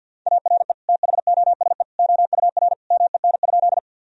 Morse code, heard as a keyed tone, about 52 wpm, 700 Hz. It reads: ARE THOSE OUR MEN?
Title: Sentences from Top 1000 Words - Review Part 1 - 50wpm